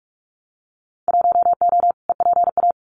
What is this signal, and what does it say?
Morse code, heard as a keyed tone, about 46 wpm, 700 Hz. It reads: 1O EPA
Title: Winter Field Day Exchanges 45wpm